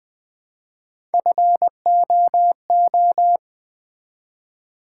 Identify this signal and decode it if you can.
Morse code, heard as a keyed tone, about 20 wpm, 700 Hz. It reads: FOO
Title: Build A Word - 1 Letter At A Time - 4-Letter Words - 20wpm